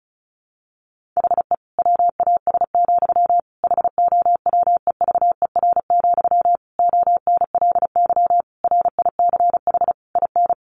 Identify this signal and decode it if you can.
Morse code, heard as a keyed tone, about 34 wpm, 700 Hz. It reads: HE WAS, HOWEVER, ONLY RICH IN
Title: Sentences from Top 1100 Words 35wpm